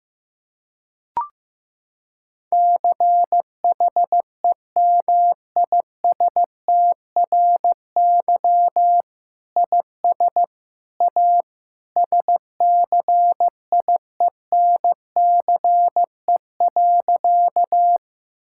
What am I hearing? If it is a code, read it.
Morse code, heard as a keyed tone, about 15 wpm, 700 Hz. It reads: CHEMISTRY IS A SCIENCE.
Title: Sentences from Top 1000 Words - Review Part 6 - 15wpm